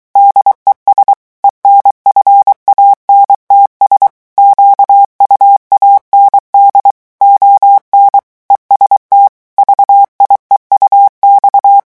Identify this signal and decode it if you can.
Morse code, heard as a keyed tone, about 23 wpm, 800 Hz. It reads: DES ENFANTS QUAND ON EST VIEUX
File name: morse mp3 / revision9 citations humoristiques vitesse 22 mots minute